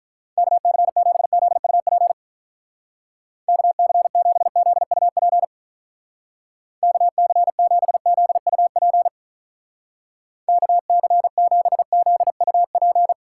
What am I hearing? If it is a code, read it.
Morse code, heard as a keyed tone, about 46 wpm, 700 Hz. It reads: KC7ZUP KC7ZUP KC7ZUP KC7ZUP